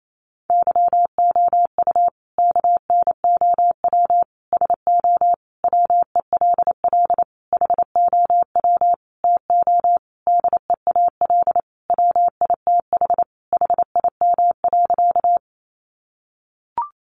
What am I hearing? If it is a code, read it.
Morse code, heard as a keyed tone, about 28 wpm, 700 Hz. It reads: YOU KNOW SO WELL HOW TO DEAL WITH HIM.
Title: Sentences from Top 800 Words 28wpm